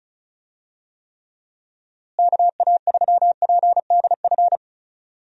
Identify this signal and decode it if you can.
Morse code, heard as a keyed tone, about 36 wpm, 700 Hz. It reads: KA3PDF
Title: USA Call Signs - 6 Characters - Repeated 3x - 35wpm